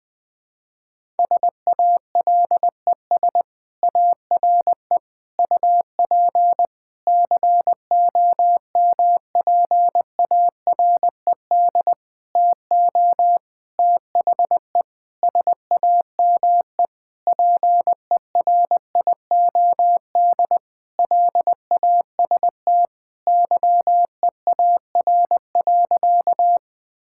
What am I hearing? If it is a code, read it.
Morse code, heard as a keyed tone, about 21 wpm, 700 Hz. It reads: SALES ARE UP COMPARED TO THE SAME PERIOD LAST YEAR.
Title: Sentences with 7-Word Phrases - 20wpm